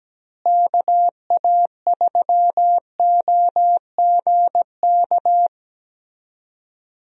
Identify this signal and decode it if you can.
Morse code, heard as a keyed tone, about 17 wpm, 700 Hz. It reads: KA3OGK
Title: USA Call Signs - 6 Characters - Repeated 3x - 17wpm